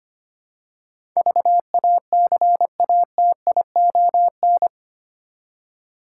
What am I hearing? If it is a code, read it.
Morse code, heard as a keyed tone, about 25 wpm, 700 Hz. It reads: VACATION
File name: Eight Letter Words - 25wpm